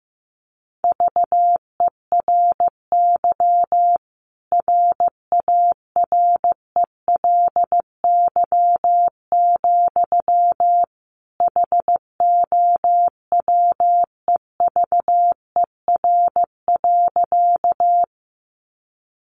Morse code, heard as a keyed tone, about 15 wpm, 700 Hz. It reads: VERY RARELY, HOWEVER.
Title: Sentences from Top 1400 Words 15wpm